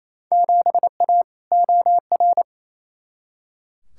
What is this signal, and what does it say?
Morse code, heard as a keyed tone, about 28 wpm, 700 Hz. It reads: 7A OR
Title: Field Day Exchanges Encore - 28wpm